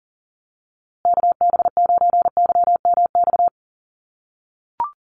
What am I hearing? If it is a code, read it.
Morse code, heard as a keyed tone, about 40 wpm, 700 Hz. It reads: KB9YMX
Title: USA Call Signs Repeated 3x 40wpm